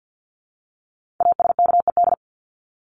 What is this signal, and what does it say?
Morse code, heard as a keyed tone, about 48 wpm, 700 Hz. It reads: ASKED
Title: ED Suffix - Top 100 Words with Base Words 50wpm